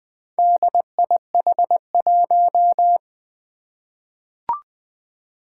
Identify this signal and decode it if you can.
Morse code, heard as a keyed tone, about 20 wpm, 700 Hz. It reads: DIH1
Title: Four Challenging Letter-Number Combinations 20wpm